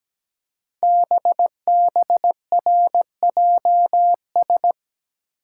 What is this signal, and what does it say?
Morse code, heard as a keyed tone, about 17 wpm, 700 Hz. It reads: BBRJS
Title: Code Groups - 5 Letters - 17wpm